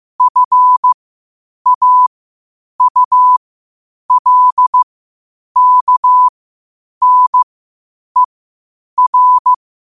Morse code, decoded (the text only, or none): FAULKNER